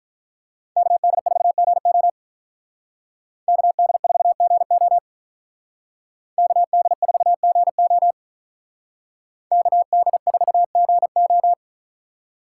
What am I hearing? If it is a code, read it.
Morse code, heard as a keyed tone, about 44 wpm, 700 Hz. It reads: KD4GO KD4GO KD4GO KD4GO
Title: Speed Racing - USA 2x2 Call Signs 35wpm